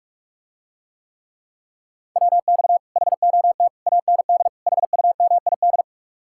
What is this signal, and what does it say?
Morse code, heard as a keyed tone, about 48 wpm, 700 Hz. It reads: WX HOT AND HUMID